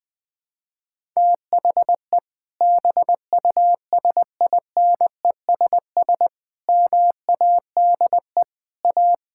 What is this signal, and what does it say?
Morse code, heard as a keyed tone, about 20 wpm, 700 Hz. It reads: THE BUSINESS MADE A
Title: Sentences from Top 1000 Words - Review Part 5 - 20wpm